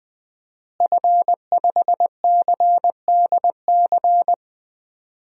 Morse code, heard as a keyed tone, about 21 wpm, 700 Hz. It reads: F5CDC
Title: DXCC-100 Call Signs - 5 Characters - Repeated 2x - 20wpm